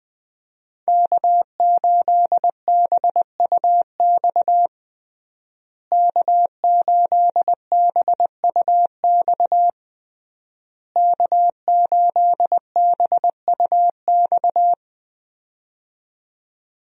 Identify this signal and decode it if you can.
Morse code, heard as a keyed tone, about 20 wpm, 700 Hz. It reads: K8BUX K8BUX K8BUX
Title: USA Call Signs - 5 Characters - Repeated 3x - 20wpm